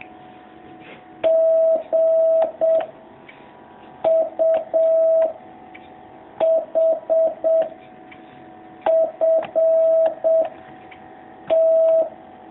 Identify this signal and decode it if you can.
Morse code, heard as a keyed tone, about 7 wpm, 650 Hz. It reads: G U H F T
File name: jh8cbh / 2019cw